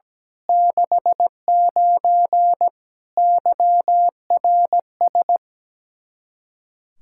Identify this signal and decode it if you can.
Morse code, heard as a keyed tone, about 17 wpm, 700 Hz. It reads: 69 YRS
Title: K1USN SST QSO Practice 17wpm